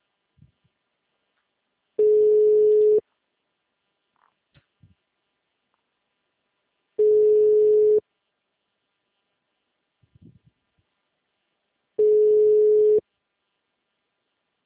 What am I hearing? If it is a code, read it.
Ringback tone (Europe, 425 Hz).